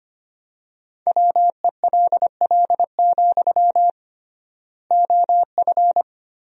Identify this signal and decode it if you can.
Morse code, heard as a keyed tone, about 25 wpm, 700 Hz. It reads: WELL, OF
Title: 3x Word-Spacing - Sentences from Top 500 Words - 25wpm